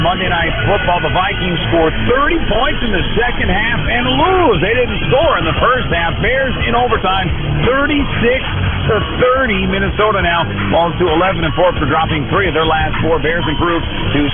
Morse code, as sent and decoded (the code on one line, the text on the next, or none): none